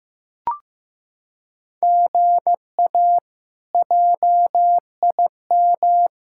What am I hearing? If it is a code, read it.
Morse code, heard as a keyed tone, about 15 wpm, 700 Hz. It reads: GA JIM
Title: K1USN SST QSO Practice - 15wpm